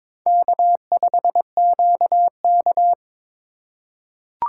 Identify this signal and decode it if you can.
Morse code, heard as a keyed tone, about 21 wpm, 700 Hz. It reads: K5QK
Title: USA Call Signs - 2x Character Spacing - 22wpm